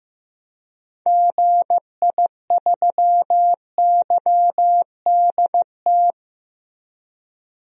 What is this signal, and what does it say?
Morse code, heard as a keyed tone, about 15 wpm, 700 Hz. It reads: GI3YDT
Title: DXCC-100 Call Signs - 6 Characters - Repeated 2x - 15wpm